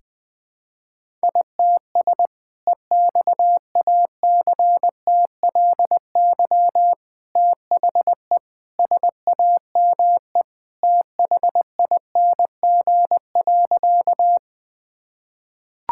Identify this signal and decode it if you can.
Morse code, heard as a keyed tone, about 21 wpm, 700 Hz. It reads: ITS EXACTLY THE SAME THING.
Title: Sentences from Top 1000 Words - Review Part 6 - 20wpm